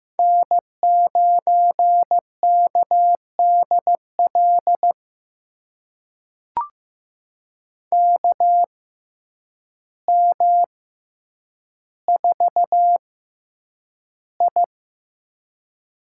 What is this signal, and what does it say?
Morse code, heard as a keyed tone, about 15 wpm, spaced as for 8 wpm, 700 Hz. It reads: N9KDL KM4I
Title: USA Call Signs - 5 Characters - 6x Character Spacing - 15wpm